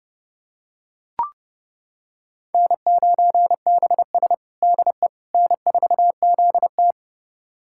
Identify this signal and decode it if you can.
Morse code, heard as a keyed tone, about 30 wpm, 700 Hz. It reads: N9BS DE N4ZT
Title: USA Callsign DE Callsign - 4 Characters - 30wpm